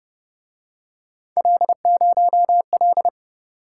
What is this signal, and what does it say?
Morse code, heard as a keyed tone, about 29 wpm, 700 Hz. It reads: L0L